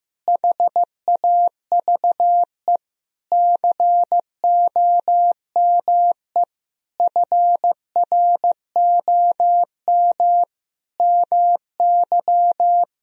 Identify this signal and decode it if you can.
Morse code, heard as a keyed tone, about 15 wpm, 700 Hz. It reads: HAVE COME FROM MY
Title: Sentences from Top 1000 Words - Review Part 4 - 15wpm